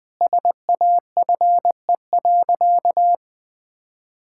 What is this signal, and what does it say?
Morse code, heard as a keyed tone, about 20 wpm, 700 Hz. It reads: SAFE.
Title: Sentences with 5-Word Phrases - 20wpm